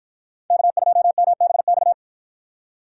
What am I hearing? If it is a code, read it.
Morse code, heard as a keyed tone, about 51 wpm, 700 Hz. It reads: K2MBX